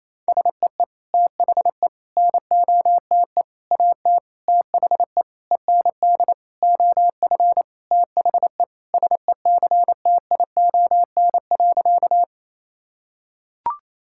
Morse code, heard as a keyed tone, about 28 wpm, 700 Hz. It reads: SEE THE NOTE AT THE END OF THE SECTION.
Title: Sentences from Top 1000 Words - Review Part 5 - 28wpm